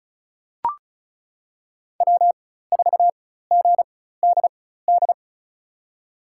Morse code, heard as a keyed tone, about 36 wpm, spaced as for 16 wpm, 700 Hz. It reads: W4GDD
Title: USA Call Signs - 5 Characters - 4x Character Spacing - 35wpm